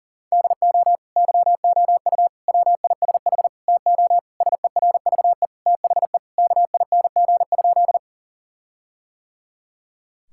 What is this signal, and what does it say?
Morse code, heard as a keyed tone, about 40 wpm, 700 Hz. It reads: DO YOU WISH TO SERVE THE KING?
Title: Sentences from Top 1600 Words 40wpm